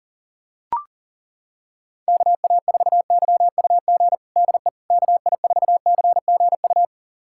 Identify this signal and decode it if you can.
Morse code, heard as a keyed tone, about 40 wpm, 700 Hz. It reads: KA4YUG DE KI4CGU
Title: USA Callsign DE Callsign - 6 Characters - 40wpm